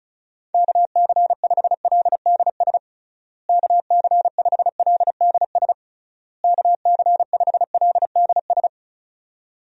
Morse code, heard as a keyed tone, about 36 wpm, 700 Hz. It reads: KC5LDS KC5LDS KC5LDS
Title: USA Call Signs - 6 Characters - Repeated 3x - 35wpm